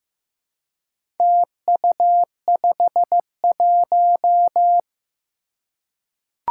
Morse code, heard as a keyed tone, about 15 wpm, 700 Hz. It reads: TU51
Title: Four Challenging Letter-Number Combinations 15wpm